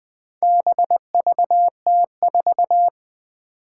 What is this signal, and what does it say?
Morse code, heard as a keyed tone, about 20 wpm, 700 Hz. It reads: BVT4
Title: Four Challenging Letter-Number Combinations 20wpm